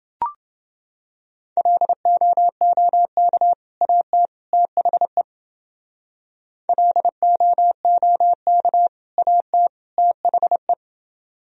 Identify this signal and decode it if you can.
Morse code, heard as a keyed tone, about 28 wpm, 700 Hz. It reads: LOOK AT THE LOOK AT THE